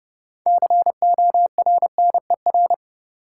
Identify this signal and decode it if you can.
Morse code, heard as a keyed tone, about 30 wpm, 700 Hz. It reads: CORNER
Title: Six Letter Words - 30wpm